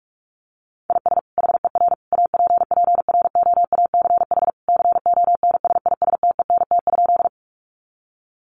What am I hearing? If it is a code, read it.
Morse code, heard as a keyed tone, about 44 wpm, 700 Hz. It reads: IS HER APPROACH CONSISTENT?
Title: Sentences from Top 1500 Words 45wpm